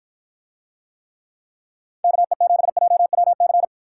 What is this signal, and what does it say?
Morse code, heard as a keyed tone, about 53 wpm, 700 Hz. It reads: KE7JWC